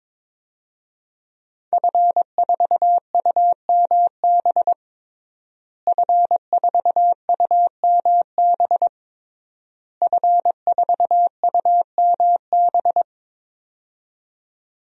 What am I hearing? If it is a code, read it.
Morse code, heard as a keyed tone, about 21 wpm, 700 Hz. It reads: F4UMB F4UMB F4UMB
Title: DXCC-100 Call Signs - 5 Characters - Repeated 3x - 22wpm